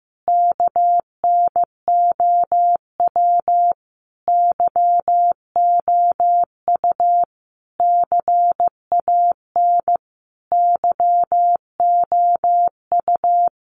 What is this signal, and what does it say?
Morse code, heard as a keyed tone, about 15 wpm, 700 Hz. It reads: KNOW YOU CAN YOU
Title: Sentences from Top 900 Words 15wpm